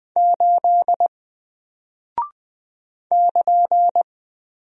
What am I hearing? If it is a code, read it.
Morse code, heard as a keyed tone, about 20 wpm, 700 Hz. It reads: 8 <KN>
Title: Single Character (Letter:Number:Punctuation:Prosign) Alt 20wpm